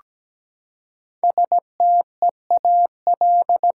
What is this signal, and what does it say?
Morse code, heard as a keyed tone, about 17 wpm, 700 Hz. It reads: STEAL
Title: Five Letter Words - 17wpm